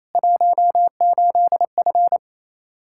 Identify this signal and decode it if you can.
Morse code, heard as a keyed tone, about 28 wpm, 700 Hz. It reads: 18F
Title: Practice QSO - Exchange 2 - Info - 28wpm